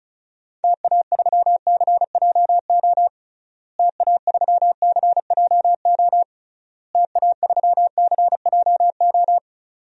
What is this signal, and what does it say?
Morse code, heard as a keyed tone, about 36 wpm, 700 Hz. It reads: TA3CJO TA3CJO TA3CJO
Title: DXCC-100 Call Signs - 6 Characters - Repeated 3x - 35wpm